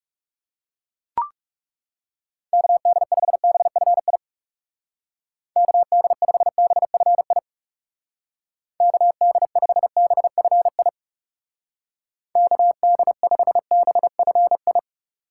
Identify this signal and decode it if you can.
Morse code, heard as a keyed tone, about 37 wpm, 700 Hz. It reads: KD5BFI KD5BFI KD5BFI KD5BFI
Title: Speed Racing - USA 2x3 Call Signs 30wpm